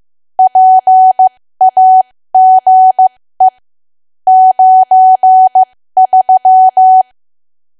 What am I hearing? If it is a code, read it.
Morse code, heard as a keyed tone, about 15 wpm, spaced as for 13 wpm, 750 Hz. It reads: PAGE 93
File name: cwa / W1AW-Level2 / 13WPM